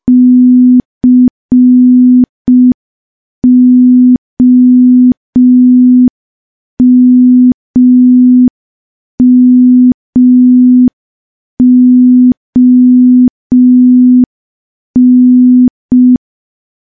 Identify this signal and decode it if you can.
Morse code, heard as a keyed tone, about 5 wpm, 250 Hz. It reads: COMMON
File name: morse / morse-training-05WPM